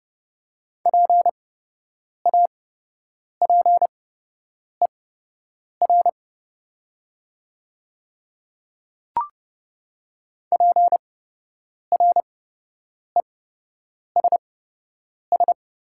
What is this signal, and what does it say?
Morse code, heard as a keyed tone, about 30 wpm, spaced as for 8 wpm, 700 Hz. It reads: PAPER PRESS
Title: 8x Character Spacing - Top 300 Words - 30wpm